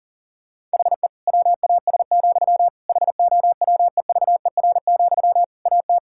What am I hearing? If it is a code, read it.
Morse code, heard as a keyed tone, about 40 wpm, 700 Hz. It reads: HE WAS, HOWEVER, AT
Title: Sentences from Top 1000 Words - Review Part 5 - 40wpm